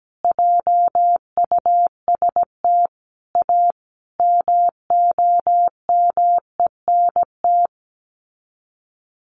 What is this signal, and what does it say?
Morse code, heard as a keyed tone, about 17 wpm, 700 Hz. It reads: JUST A MOMENT
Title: Sets of 5 Words Encore 17wpm